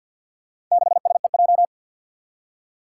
Morse code, heard as a keyed tone, about 53 wpm, 700 Hz. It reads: 6SEJ
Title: Four Challenging Letter-Number Combinations 50wpm